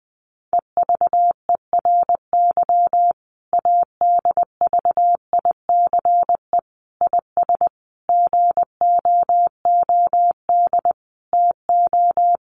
Morse code, heard as a keyed tone, about 20 wpm, 700 Hz. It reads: EVERY ADVICE IS GOOD TO
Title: Sentences from Top 1100 Words 20wpm